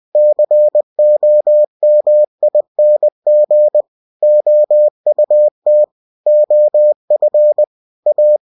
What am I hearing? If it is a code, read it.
Morse code, heard as a keyed tone, about 20 wpm, 600 Hz. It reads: COMING OUT OF A
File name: cw-books / winnie-the-pooh / winnie-20wpm-page-0020